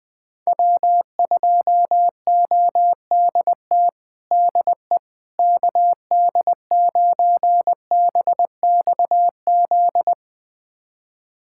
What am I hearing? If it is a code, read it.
Morse code, heard as a keyed tone, about 20 wpm, 700 Hz. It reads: W2ODT DE KD9BXZ
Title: USA Callsign DE Callsign 20wpm